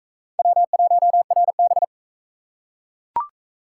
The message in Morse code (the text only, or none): W1RB